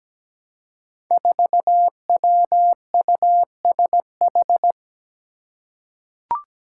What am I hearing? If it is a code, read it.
Morse code, heard as a keyed tone, about 17 wpm, 700 Hz. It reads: 4WUSH
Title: Five Challenging Letter-Number Combinations 17wpm